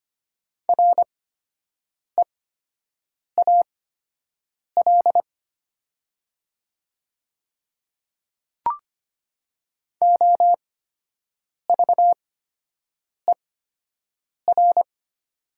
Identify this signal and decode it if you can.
Morse code, heard as a keyed tone, about 25 wpm, spaced as for 7 wpm, 700 Hz. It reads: REAL OVER
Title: 8x Character Spacing - Top 300 Words - 25wpm